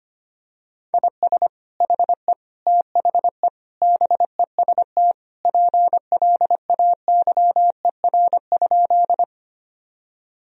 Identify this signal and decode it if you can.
Morse code, heard as a keyed tone, about 25 wpm, 700 Hz. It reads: IS HE THE BEST PLAYER?
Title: Sentences from Top 1000 Words - Review Part 4 - 25wpm